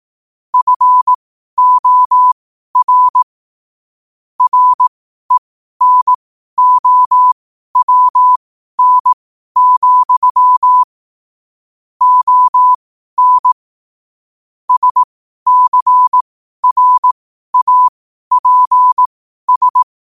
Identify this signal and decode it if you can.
Morse code, heard as a keyed tone, about 18 wpm, spaced as for 12 wpm, 1000 Hz. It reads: FOR RENOWN, ON SCRAPS
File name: qotdfiles / 12wpm